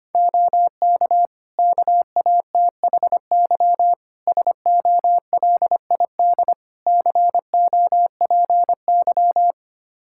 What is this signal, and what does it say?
Morse code, heard as a keyed tone, about 25 wpm, 700 Hz. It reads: OK KATHY SOLID COPY
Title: Practice QSO - Exchange 3 - More Info - 25wpm